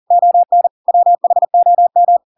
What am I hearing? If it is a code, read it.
Morse code, heard as a keyed tone, about 40 wpm, 700 Hz. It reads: ON WHOM